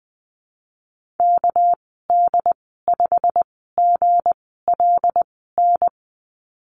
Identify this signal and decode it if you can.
Morse code, heard as a keyed tone, about 19 wpm, spaced as for 14 wpm, 700 Hz. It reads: KD5GLN
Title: 207 Lesson 33g - Letter j - Call Signs 20wpm